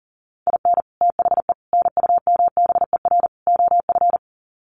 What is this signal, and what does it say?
Morse code, heard as a keyed tone, about 37 wpm, 700 Hz. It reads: IN THE NUMBER OF
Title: Sets of 4 Words 40wpm